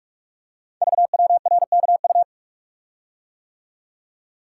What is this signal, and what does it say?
Morse code, heard as a keyed tone, about 44 wpm, 700 Hz. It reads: VWRKU
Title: Code Groups - 5 Letter:Numbers - 45wpm